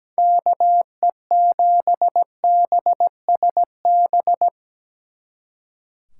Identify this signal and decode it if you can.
Morse code, heard as a keyed tone, about 20 wpm, 700 Hz. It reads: KE7BSB
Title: Speed Racing - USA 2x3 Call Signs 17wpm